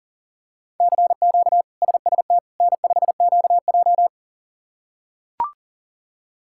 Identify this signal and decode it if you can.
Morse code, heard as a keyed tone, about 40 wpm, 690 Hz. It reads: CQ SST N5QJ
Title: K1USN SST QSO Practice - 40wpm